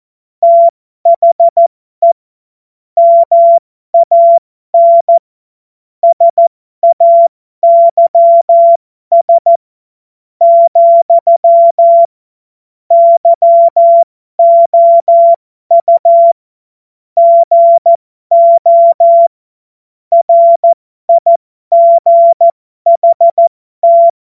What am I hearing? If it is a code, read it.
Morse code, heard as a keyed tone, about 14 wpm, spaced as for 12 wpm, 680 Hz. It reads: THE MAN SAYS, YOU GO RIGHT
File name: Morsekurs / Witze